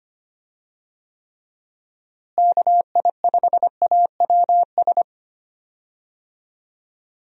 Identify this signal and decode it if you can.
Morse code, heard as a keyed tone, about 25 wpm, 700 Hz. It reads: KI5AWS
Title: USA Call Signs - 6 Characters - Repeated 2x - 25wpm